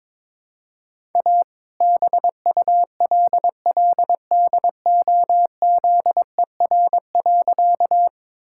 Morse code, heard as a keyed tone, about 22 wpm, 700 Hz. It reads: A BULLDOZER.
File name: All-Age Appropriate Jokes - 22wpm